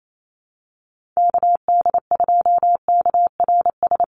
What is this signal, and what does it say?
Morse code, heard as a keyed tone, about 28 wpm, 700 Hz. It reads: KD2KRS
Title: USA Call Signs Repeated 2x 28wpm